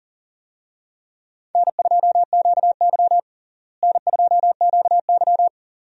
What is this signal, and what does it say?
Morse code, heard as a keyed tone, about 38 wpm, 700 Hz. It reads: N2QY N2QY
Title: USA Call Signs - 4 Characters - Repeated 2x - 40wpm